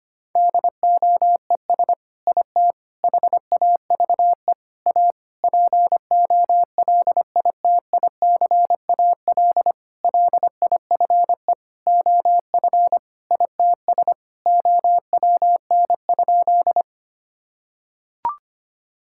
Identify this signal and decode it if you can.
Morse code, heard as a keyed tone, about 25 wpm, 700 Hz. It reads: DOES IT HAVE A POLITICAL LIFE OF ITS OWN?
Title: Sentences from Top 1000 Words - Review Part 5 - 25wpm